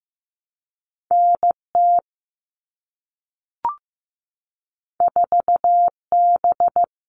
Morse code, heard as a keyed tone, about 15 wpm, 700 Hz. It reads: NT 4B
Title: All Cut Numbers - 2-Digit 15wpm